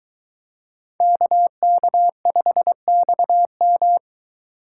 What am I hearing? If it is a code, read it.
Morse code, heard as a keyed tone, about 22 wpm, 700 Hz. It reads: KK5XM